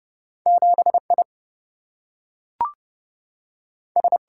Morse code, heard as a keyed tone, about 30 wpm, 700 Hz. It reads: 7 I S
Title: Single-Letter Single-Number 30wpm